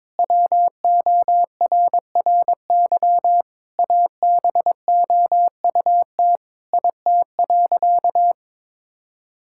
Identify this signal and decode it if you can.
Morse code, heard as a keyed tone, about 22 wpm, 700 Hz. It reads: WORRY ABOUT IT.
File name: Sentences with 7-Word Phrases - 22wpm